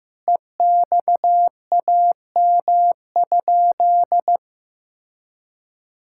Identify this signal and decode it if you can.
Morse code, heard as a keyed tone, about 16 wpm, 700 Hz. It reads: EXAM?
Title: Sentences from Top 1000 Words - Review Part 5 - 15wpm